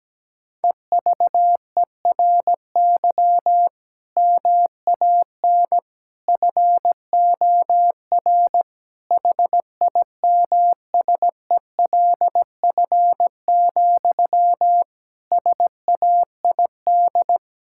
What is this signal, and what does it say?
Morse code, heard as a keyed tone, about 17 wpm, 700 Hz. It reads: EVERY MAN FOR HIMSELF, SAID
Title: Sentences from Top 1600 Words 17wpm